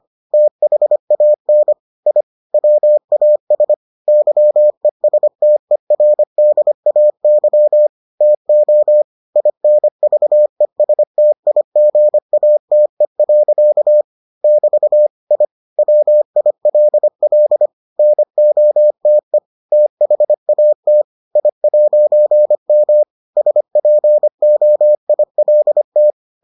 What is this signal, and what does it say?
Morse code, heard as a keyed tone, about 25 wpm, 600 Hz. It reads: THAN I WAS YESTERDAY TO INVESTIGATE. <BT> I WILL NOTE THAT I'M SPOILT